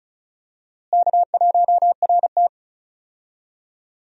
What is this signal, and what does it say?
Morse code, heard as a keyed tone, about 34 wpm, 700 Hz. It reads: K1RT
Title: USA Call Signs - 4 Characters - Repeated 3x - 35wpm